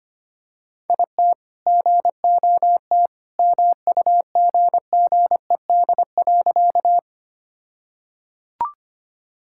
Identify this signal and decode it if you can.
Morse code, heard as a keyed tone, about 25 wpm, 700 Hz. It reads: IT GOT MUGGED.
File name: practice / 25wpm / All-Age Appropriate Jokes - 25wpm